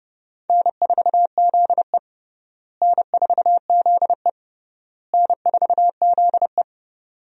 Morse code, heard as a keyed tone, about 32 wpm, 700 Hz. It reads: N4ZE N4ZE N4ZE
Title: USA Call Signs - 4 Characters - Repeated 3x - 30wpm